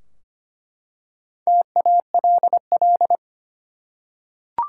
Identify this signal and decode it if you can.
Morse code, heard as a keyed tone, about 25 wpm, 700 Hz. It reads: TALL
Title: Four Letter Words - 25wpm